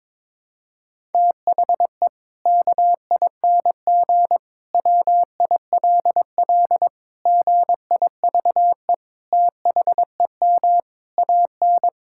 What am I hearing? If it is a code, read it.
Morse code, heard as a keyed tone, about 22 wpm, 700 Hz. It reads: THE KING WILL GIVE THEM AN
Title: Sentences from Top 1000 Words - Review Part 5 - 22wpm